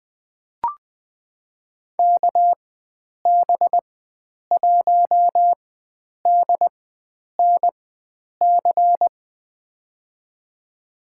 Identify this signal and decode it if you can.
Morse code, heard as a keyed tone, about 20 wpm, spaced as for 9 wpm, 700 Hz. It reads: KB1DNC
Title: USA Call Signs - 6 Characters - 4x Character Spacing - 20wpm